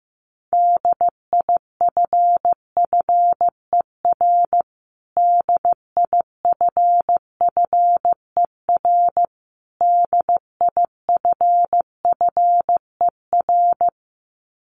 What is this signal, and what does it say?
Morse code, heard as a keyed tone, about 15 wpm, 700 Hz. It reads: DIFFER DIFFER DIFFER
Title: Top 500 Words Repeated 15wpm